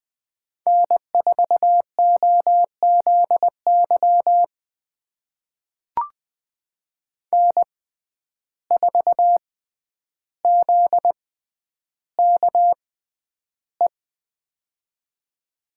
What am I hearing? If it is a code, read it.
Morse code, heard as a keyed tone, about 20 wpm, spaced as for 10 wpm, 700 Hz. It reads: N4OZY N4ZKE
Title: USA Call Signs - 5 Characters - 6x Character Spacing - 20wpm